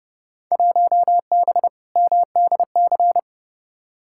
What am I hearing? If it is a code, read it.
Morse code, heard as a keyed tone, about 31 wpm, 700 Hz. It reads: 1B MDC
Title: Field Day Exchanges Encore - 30wpm